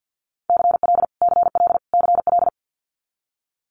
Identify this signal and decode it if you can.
Morse code, heard as a keyed tone, about 50 wpm, 700 Hz. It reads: CL CL CL
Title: Multi-letter Phonemes Repeated 3x 50wpm